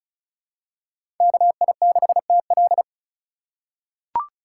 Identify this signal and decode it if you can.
Morse code, heard as a keyed tone, about 34 wpm, 700 Hz. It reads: KI6TL
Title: USA Call Signs - 5 Characters - 6x Character Spacing - 35wpm